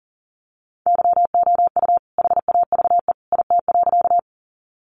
Morse code, heard as a keyed tone, about 37 wpm, 700 Hz. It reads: YOU HAVE IT.
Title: Sentences from Top 100 Words 40wpm